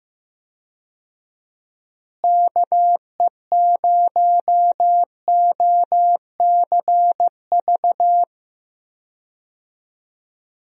Morse code, heard as a keyed tone, about 15 wpm, 700 Hz. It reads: KE0OCV
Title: USA Call Signs - 6 Characters - Repeated 2x - 15wpm